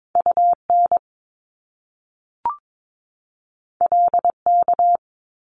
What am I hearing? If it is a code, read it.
Morse code, heard as a keyed tone, about 22 wpm, 700 Hz. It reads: UN LK
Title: Top Two Letter Combinations in Words 22wpm